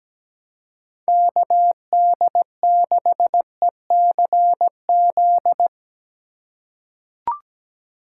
Morse code, heard as a keyed tone, about 17 wpm, 700 Hz. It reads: KD6ECZ
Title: USA Call Signs - 6 Characters - 2x Character Spacing - 17wpm